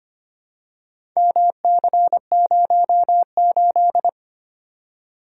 Morse code, heard as a keyed tone, about 25 wpm, 700 Hz. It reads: MC08
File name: practice / 25wpm / Four Letter-Number Combinations - 25wpm